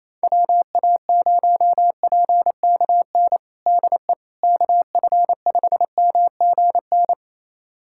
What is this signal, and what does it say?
Morse code, heard as a keyed tone, about 27 wpm, 700 Hz. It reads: WA0PKN DE KF5MGN
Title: USA Callsign DE Callsign - 6 Characters - 28wpm